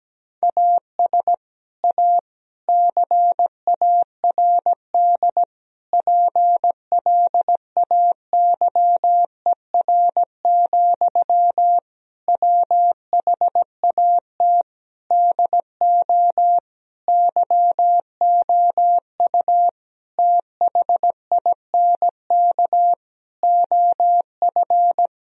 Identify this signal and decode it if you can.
Morse code, heard as a keyed tone, about 17 wpm, 700 Hz. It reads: AS A CARD PLAYER, WHAT DO YOU THINK OF
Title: Sentences from Top 1000 Words - Review Part 4 - 17wpm